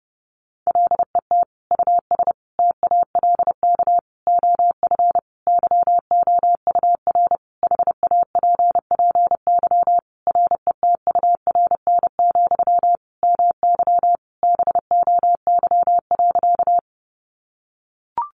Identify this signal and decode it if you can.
Morse code, heard as a keyed tone, about 28 wpm, 700 Hz. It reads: LET US TALK OF YOUR HAPPY RETURN, MY BOY.
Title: Sentences from Top 1000 Words 30wpm